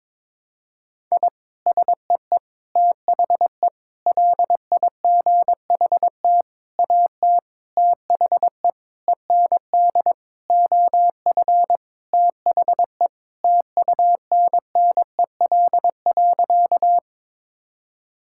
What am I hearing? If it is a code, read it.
Morse code, heard as a keyed tone, about 22 wpm, 700 Hz. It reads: I SEE THE LIGHT AT THE END OF THE TUNNEL.
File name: practice / 22wpm / Sentences with 8-Word Phrases - 22wpm